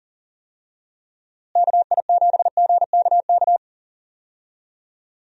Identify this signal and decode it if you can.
Morse code, heard as a keyed tone, about 40 wpm, 700 Hz. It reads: KI7GKK
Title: USA Call Signs - 6 Characters - Repeated 3x - 40wpm